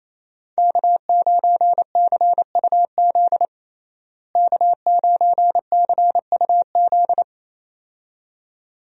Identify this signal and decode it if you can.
Morse code, heard as a keyed tone, about 28 wpm, 700 Hz. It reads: K9CUZ K9CUZ
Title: USA Call Signs - 5 Characters - Repeated 2x - 28wpm